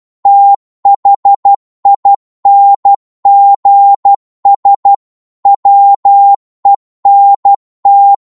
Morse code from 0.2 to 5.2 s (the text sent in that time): THINGS